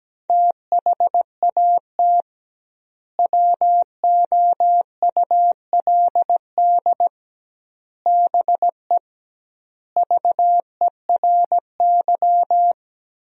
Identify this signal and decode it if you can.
Morse code, heard as a keyed tone, about 17 wpm, 700 Hz. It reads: THAT WOULD BE VERY
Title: 2x Word-Spacing - Sentences from Top 200 Words - 17wpm